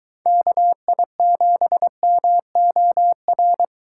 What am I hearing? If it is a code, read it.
Morse code, heard as a keyed tone, about 23 wpm, 700 Hz. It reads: KI7MOR